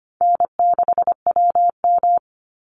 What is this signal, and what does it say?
Morse code, heard as a keyed tone, about 25 wpm, 700 Hz. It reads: N6WM
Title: Easy Call Signs - 25wpm